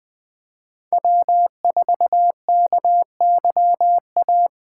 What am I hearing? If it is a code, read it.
Morse code, heard as a keyed tone, about 21 wpm, 700 Hz. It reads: W4KYA
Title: USA Call Signs - 5 Characters - 3x Character Spacing - 20wpm